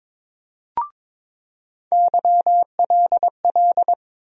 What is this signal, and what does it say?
Morse code, heard as a keyed tone, about 22 wpm, 700 Hz. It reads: YLL
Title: USA Call Sign Postfixes - 3 Letters 22wpm